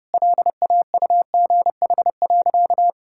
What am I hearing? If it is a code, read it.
Morse code, heard as a keyed tone, about 30 wpm, 700 Hz. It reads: LAUGH.
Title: Sentences from Top 1000 Words - Review Part 5 - 30wpm